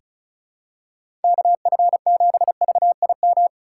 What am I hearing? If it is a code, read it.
Morse code, heard as a keyed tone, about 36 wpm, 700 Hz. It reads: KF7VIM
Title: USA Call Signs - 6 Characters - 2x Character Spacing - 35wpm